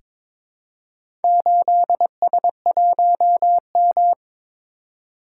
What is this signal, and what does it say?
Morse code, heard as a keyed tone, about 22 wpm, 700 Hz. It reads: 8S1M
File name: practice / 22wpm / Four Letter-Number Combinations - 22wpm